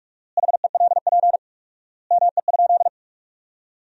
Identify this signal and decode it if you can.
Morse code, heard as a keyed tone, about 45 wpm, 700 Hz. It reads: HELP ME?
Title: 4x Word-Spacing - Sentences from Top 200 Words - 45wpm